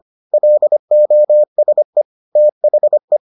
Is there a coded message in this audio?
Yes: Morse code from 0.3 to 3.2 s, heard as a keyed tone, about 25 wpm, 600 Hz. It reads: LOSE THE